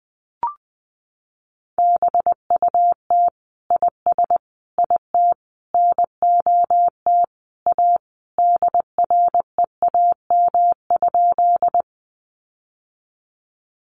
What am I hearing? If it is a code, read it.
Morse code, heard as a keyed tone, about 20 wpm, 700 Hz. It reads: BUT IS IT NOT A DREAM?
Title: Sentences from Top 1500 Words 20wpm